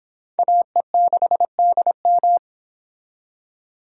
Morse code, heard as a keyed tone, about 25 wpm, 700 Hz. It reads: AE6DM